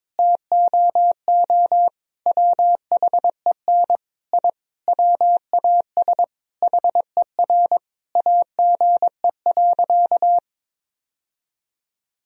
Morse code, heard as a keyed tone, about 21 wpm, 700 Hz. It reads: TOO WHEN I WAS HER AGE.
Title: Sentences from Top 1000 Words - Review Part 3 - 22wpm